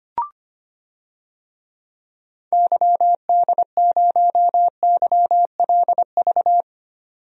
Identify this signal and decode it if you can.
Morse code, heard as a keyed tone, about 25 wpm, 700 Hz. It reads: YD0YLV
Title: DXCC-100 Call Signs - 6 Characters - 25wpm